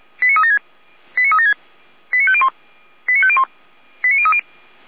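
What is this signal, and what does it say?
Five-tone selective call (ZVEI), heard as quick runs of five tones, tones of about 70 ms. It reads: 89376 89376 89501 89501 89030
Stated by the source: ZVEI five-tone selective call received with an SDR.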